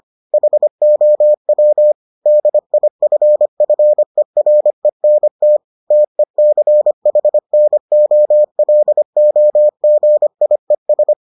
Morse code, heard as a keyed tone, about 25 wpm, 600 Hz. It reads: HOW DIFFERENT TECHNOLOGIES